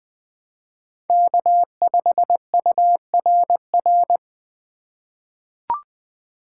Morse code, heard as a keyed tone, about 20 wpm, 700 Hz. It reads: K5URR
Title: USA Call Signs - 3x Character Spacing - 20wpm